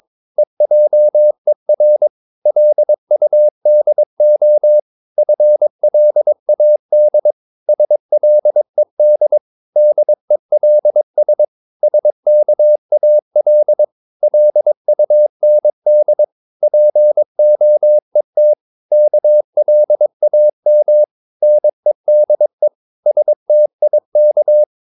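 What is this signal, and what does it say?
Morse code, heard as a keyed tone, about 22 wpm, 600 Hz. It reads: EJER LUDO FLAD SLED DELS SKAL LUND POET KLAM NEDE STIK
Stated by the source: Morse practice of short Danish words at 22 wpm, from CW med Gnister.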